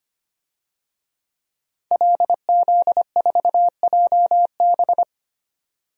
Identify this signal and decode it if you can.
Morse code, heard as a keyed tone, about 25 wpm, 700 Hz. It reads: LZ4JB
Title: DXCC-100 Call Signs - 5 Characters - 25wpm